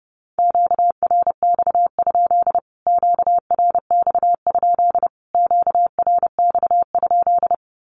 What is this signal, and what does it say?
Morse code, heard as a keyed tone, about 30 wpm, 700 Hz. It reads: QRX? QRX? QRX?
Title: Q-Codes Repeated 3x 30wpm